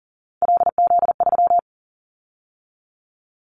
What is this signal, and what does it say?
Morse code, heard as a keyed tone, about 40 wpm, 700 Hz. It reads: LZ3
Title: Callsign Prefixes for Top 50 DXCC - 40wpm